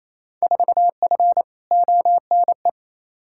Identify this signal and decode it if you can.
Morse code, heard as a keyed tone, about 28 wpm, 700 Hz. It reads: 4F ONE
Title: Field Day Exchanges Encore - 28wpm